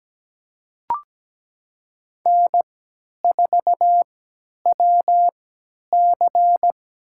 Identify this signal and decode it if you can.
Morse code, heard as a keyed tone, about 17 wpm, spaced as for 10 wpm, 700 Hz. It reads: N4WC
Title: USA Call Signs - 3x Character Spacing - 17wpm